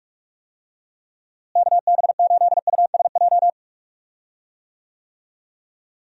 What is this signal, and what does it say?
Morse code, heard as a keyed tone, about 44 wpm, 700 Hz. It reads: KB8USJ
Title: USA Call Signs - 6 Characters - Repeated 3x - 45wpm